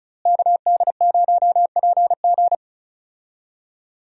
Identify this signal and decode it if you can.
Morse code, heard as a keyed tone, about 36 wpm, 700 Hz. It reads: KD0PG
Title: USA Call Signs - 5 Characters - Repeated 3x - 35wpm